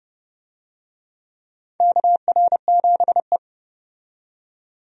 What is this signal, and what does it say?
Morse code, heard as a keyed tone, about 32 wpm, 700 Hz. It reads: KR7E
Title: USA Call Signs - 4 Characters - 30wpm